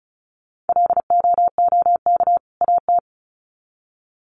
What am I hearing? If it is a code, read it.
Morse code, heard as a keyed tone, about 34 wpm, 700 Hz. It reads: LOOK AT
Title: Sets of 2 Words 35wpm